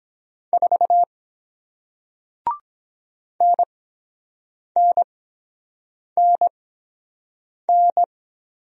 Morse code, heard as a keyed tone, about 22 wpm, spaced as for 7 wpm, 700 Hz. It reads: 4 NNNN